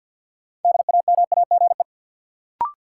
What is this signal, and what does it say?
Morse code, heard as a keyed tone, about 52 wpm, 700 Hz. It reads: DAMAGE
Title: Sentences from Top 1100 Words - New Word Drill Alt - 50wpm